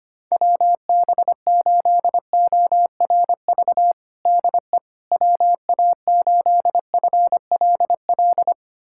Morse code, heard as a keyed tone, about 25 wpm, 700 Hz. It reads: WB8ORV DE WA8FLL
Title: USA Callsign DE Callsign - 6 Characters - 25wpm